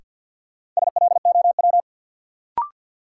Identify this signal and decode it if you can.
Morse code, heard as a keyed tone, about 53 wpm, 700 Hz. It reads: SLOW
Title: Sentences from Top 400 Words - New Word Drill - 50wpm